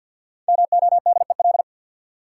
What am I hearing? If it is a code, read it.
Morse code, heard as a keyed tone, about 53 wpm, 700 Hz. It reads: MODEL
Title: Five Letter Words - 50wpm